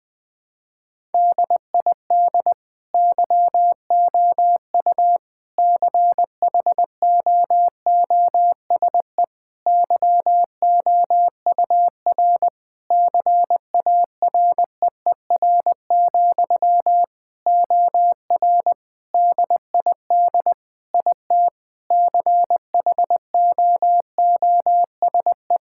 Morse code, heard as a keyed tone, about 20 wpm, 700 Hz. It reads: DID YOU CHOOSE YOUR CAREER, OR DID IT CHOOSE
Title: Sentences from Top 1000 Words - Review Part 6 - 20wpm